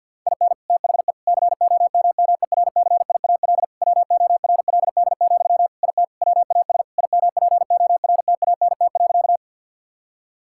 Morse code, heard as a keyed tone, about 50 wpm, 700 Hz. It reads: IN THE COMMERCIAL WORLD, IT WAS IMPORTANT.
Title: Sentences from Top 1000 Words - Review Part 5 - 50wpm